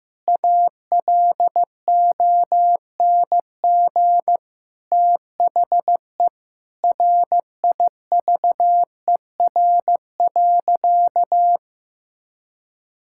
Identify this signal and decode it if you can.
Morse code, heard as a keyed tone, about 15 wpm, 700 Hz. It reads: ALONG THE RIVER.
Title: Sentences from Top 1000 Words - Review Part 6 - 15wpm